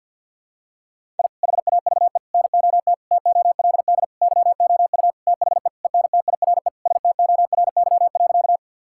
Morse code, heard as a keyed tone, about 52 wpm, 700 Hz. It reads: I HAVE NOT TOLD YOU THE ENTIRE STORY.
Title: Sentences from Top 1000 Words - Review Part 5 - 50wpm